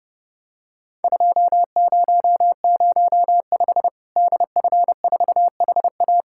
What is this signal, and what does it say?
Morse code, heard as a keyed tone, about 30 wpm, 700 Hz. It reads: 2005 DF4HA
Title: Worked All Europe DX Contest - QTC Exchange - 30wpm